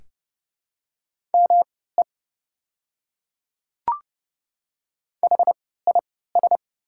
Morse code, heard as a keyed tone, about 29 wpm, spaced as for 17 wpm, 690 Hz. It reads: ME HIS
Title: 3x Character Spacing - Top 300 Words - 30wpm